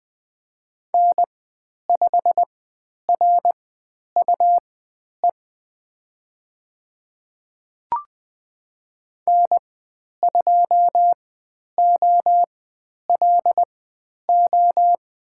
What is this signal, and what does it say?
Morse code, heard as a keyed tone, about 20 wpm, spaced as for 10 wpm, 700 Hz. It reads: N5RUE N2OLO
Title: 102 Lesson 17g - Letter u - Call Signs 20wpm